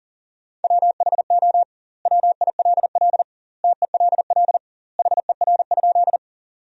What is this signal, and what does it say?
Morse code, heard as a keyed tone, about 40 wpm, 700 Hz. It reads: WHO WILL TELL HER?
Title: 2x Word-Spacing - Sentences from Top 200 Words - 40wpm